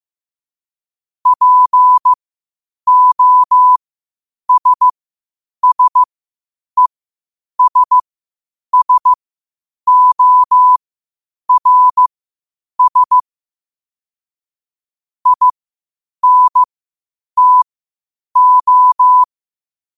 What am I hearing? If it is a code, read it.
Morse code, heard as a keyed tone, about 15 wpm, spaced as for 8 wpm, 1000 Hz. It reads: POSSESSORS INTO
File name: qotdfiles / 8wpm